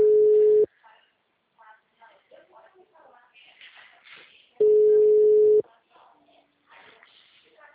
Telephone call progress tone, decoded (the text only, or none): ringback tone